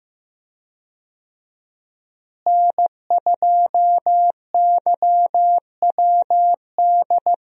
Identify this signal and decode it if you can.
Morse code, heard as a keyed tone, about 15 wpm, 700 Hz. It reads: N2YWD
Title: USA Call Signs - 5 Characters - Repeated 3x - 15wpm